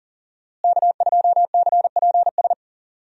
Morse code, heard as a keyed tone, about 40 wpm, 700 Hz. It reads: K2CPS
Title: USA Call Signs - 5 Characters - Repeated 3x - 40wpm